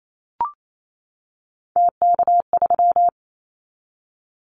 Morse code, heard as a keyed tone, about 28 wpm, 700 Hz. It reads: TK3
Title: Callsign Prefixes for Top 100 DXCC - 28wpm